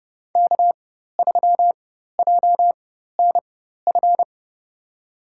Morse code, heard as a keyed tone, about 31 wpm, spaced as for 14 wpm, 700 Hz. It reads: K3JNF
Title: USA Call Signs - 4x Character Spacing - 30wpm